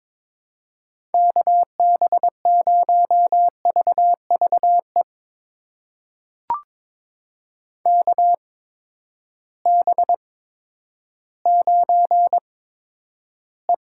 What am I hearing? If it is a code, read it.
Morse code, heard as a keyed tone, about 22 wpm, spaced as for 11 wpm, 700 Hz. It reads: KB0VVE KB9E
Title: USA Call Signs - 8x Character Spacing - 22wpm